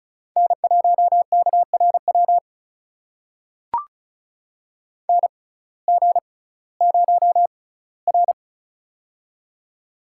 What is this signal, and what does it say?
Morse code, heard as a keyed tone, about 34 wpm, spaced as for 19 wpm, 700 Hz. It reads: N1KRW NG0R
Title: USA Call Signs - 6x Character Spacing - 35wpm